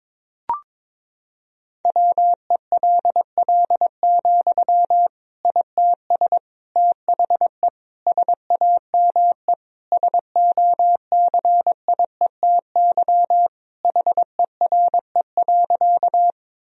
Morse code, heard as a keyed tone, about 22 wpm, 700 Hz. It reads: WELL, ITS THE SAME SOCIETY HERE.
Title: Sentences from Top 1000 Words - Review Part 4 - 22wpm